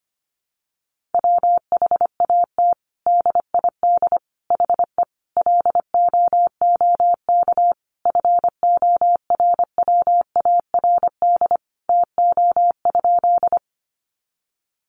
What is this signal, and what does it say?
Morse code, heard as a keyed tone, about 25 wpm, 700 Hz. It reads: WHAT DID HE LOOK FORWARD TO?
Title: Sentences from Top 1300 Words 25wpm